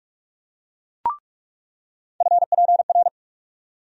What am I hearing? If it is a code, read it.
Morse code, heard as a keyed tone, about 41 wpm, 700 Hz. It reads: FPR
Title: USA Call Sign Postfixes - 3 Letters 45wpm